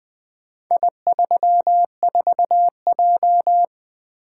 Morse code, heard as a keyed tone, about 20 wpm, 700 Hz. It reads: I34J
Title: Four Letter-Number Combinations - 20wpm